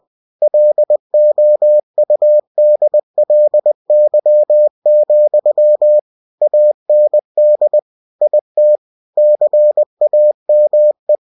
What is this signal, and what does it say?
Morse code, heard as a keyed tone, about 20 wpm, 600 Hz. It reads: LOUDLY, AND IT CAME
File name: cw-books / winnie-the-pooh / winnie-20wpm-page-0032